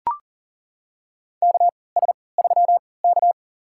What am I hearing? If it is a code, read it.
Morse code, heard as a keyed tone, about 40 wpm, spaced as for 23 wpm, 700 Hz. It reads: KS3K
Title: USA Call Signs - 4 Characters - 3x Character Spacing - 40wpm